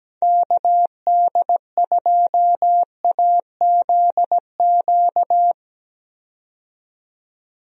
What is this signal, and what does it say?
Morse code, heard as a keyed tone, about 17 wpm, 700 Hz. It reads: KD2AZQ
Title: USA Call Signs - 6 Characters - 17wpm